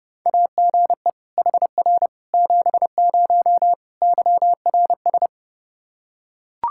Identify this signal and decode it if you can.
Morse code, heard as a keyed tone, about 30 wpm, 700 Hz. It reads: AGE HR 70 YRS
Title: Practice QSO - Exchange 3 - More Info - 30wpm